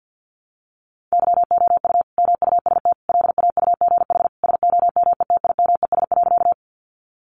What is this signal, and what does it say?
Morse code, heard as a keyed tone, about 50 wpm, 700 Hz. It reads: YOU MUST LAUGH SOMETIMES.
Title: Sentences from Top 600 Words 50wpm